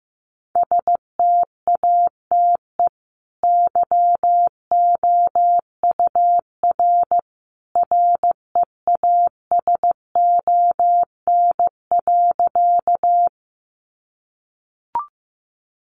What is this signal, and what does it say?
Morse code, heard as a keyed tone, about 15 wpm, 700 Hz. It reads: STATE YOUR REASON.
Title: Sentences from Top 700 Words 15wpm